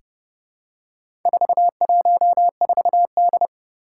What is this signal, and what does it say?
Morse code, heard as a keyed tone, about 30 wpm, 700 Hz. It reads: 414D
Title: Four Challenging Letter-Number Combinations 30wpm